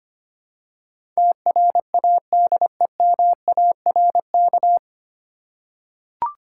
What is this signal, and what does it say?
Morse code, heard as a keyed tone, about 25 wpm, 700 Hz. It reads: TRADEMARK
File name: practice / 25wpm / 9 Letter Words - First 3 Letters Given - 25wpm